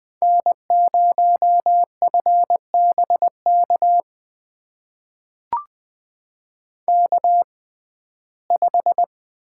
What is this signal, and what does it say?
Morse code, heard as a keyed tone, about 20 wpm, spaced as for 15 wpm, 700 Hz. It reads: N0FBK K5
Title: USA Call Signs - 5 Characters - 6x Character Spacing - 20wpm